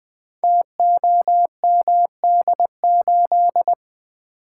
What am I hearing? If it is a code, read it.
Morse code, heard as a keyed tone, about 20 wpm, 700 Hz. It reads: TOMD8
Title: Code Groups - 5 Letter:Numbers - 20wpm